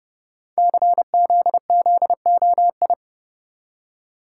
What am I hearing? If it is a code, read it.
Morse code, heard as a keyed tone, about 30 wpm, 700 Hz. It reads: CZZOI
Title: Code Groups - 5 Letters - 30wpm